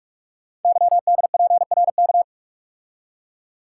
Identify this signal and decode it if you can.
Morse code, heard as a keyed tone, about 44 wpm, 700 Hz. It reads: YDPRK
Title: Code Groups - 5 Letters - 45wpm